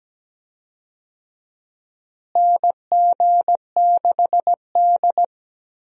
Morse code, heard as a keyed tone, about 17 wpm, 700 Hz. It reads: NG6D
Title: USA Call Signs - 4 Characters - Repeated 3x - 17wpm